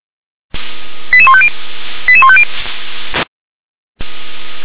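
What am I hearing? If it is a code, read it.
Five-tone selective call (ZVEI), heard as quick runs of five tones, tones of about 70 ms. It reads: 88150 88150
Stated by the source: ZVEI five-tone selective call received on an SDR.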